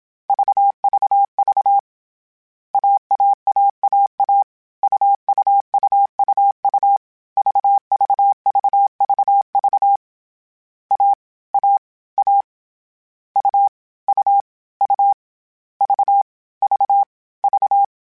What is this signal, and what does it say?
Morse code, heard as a keyed tone, about 27 wpm, 800 Hz. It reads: VVV AAAAA UUUUU VVVVV A A A U U U V V V
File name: Short Runs / Short Runs-mp3s / AUV AUV4 @25